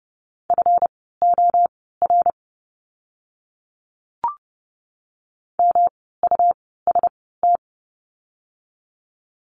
Morse code, heard as a keyed tone, about 30 wpm, spaced as for 17 wpm, 700 Hz. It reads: FOR MUST
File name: course / 30:10wpm / 237 Lesson 38f - Letter q - Words 30wpm